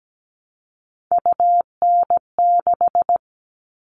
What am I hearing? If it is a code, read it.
Morse code, heard as a keyed tone, about 17 wpm, 700 Hz. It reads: UN6
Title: All Cut Numbers - 3-Digit 17wpm